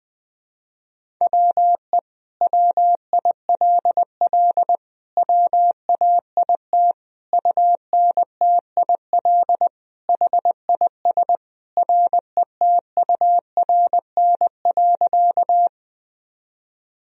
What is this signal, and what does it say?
Morse code, heard as a keyed tone, about 20 wpm, 700 Hz. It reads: WE WILL WAIT UNTIL HIS RETURN.
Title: Sentences from Top 1000 Words - Review Part 1 - 20wpm